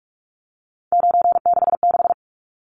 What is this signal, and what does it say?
Morse code, heard as a keyed tone, about 44 wpm, 700 Hz. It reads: 966
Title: Numbers - 3-Digit 45wpm